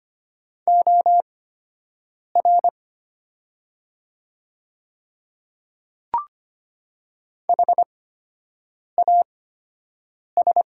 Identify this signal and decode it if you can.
Morse code, heard as a keyed tone, about 25 wpm, spaced as for 7 wpm, 700 Hz. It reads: OR HAS
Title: 8x Character Spacing - Top 300 Words - 25wpm